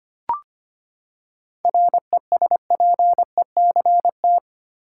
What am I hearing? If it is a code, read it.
Morse code, heard as a keyed tone, about 25 wpm, 700 Hz. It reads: RESPECT
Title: Seven Letter Words - 25wpm